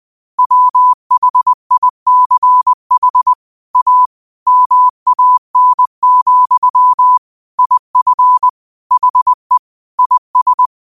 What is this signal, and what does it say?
Morse code, heard as a keyed tone, about 20 wpm, 1000 Hz. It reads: WHICH A MAN, IF HE IS